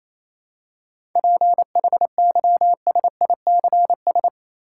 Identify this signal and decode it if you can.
Morse code, heard as a keyed tone, about 28 wpm, 700 Hz. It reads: PHYSICS
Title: Seven Letter Words - 28wpm